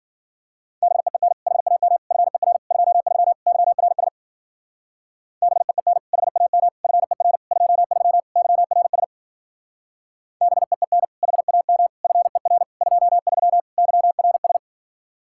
Morse code, heard as a keyed tone, about 52 wpm, 700 Hz. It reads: BEEN HAM FER 23 YRS BEEN HAM FER 23 YRS BEEN HAM FER 23 YRS